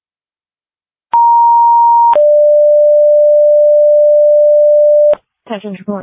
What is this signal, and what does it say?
Two-tone page, heard as a one-second tone, then a three-second tone, about 940 then 590 Hz, with speech. It is heard from 1.1 to 5.1 s.